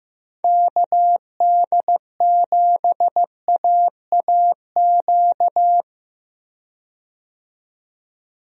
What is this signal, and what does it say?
Morse code, heard as a keyed tone, about 15 wpm, 700 Hz. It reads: KD7AAQ
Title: USA Call Signs - 6 Characters - 15wpm